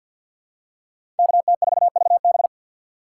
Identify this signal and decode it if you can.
Morse code, heard as a keyed tone, about 51 wpm, 700 Hz. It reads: KT4VB
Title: USA Call Signs - 5 Characters - 2x Character Spacing - 50wpm